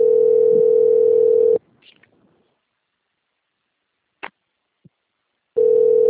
Ringback tone (North America), 440 + 480 Hz.
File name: out- - - - - .9786